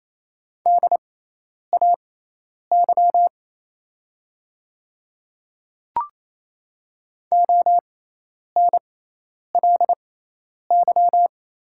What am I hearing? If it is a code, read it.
Morse code, heard as a keyed tone, about 28 wpm, spaced as for 10 wpm, 700 Hz. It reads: DAY ONLY
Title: 6x Character Spacing - Top 300 Words - 28wpm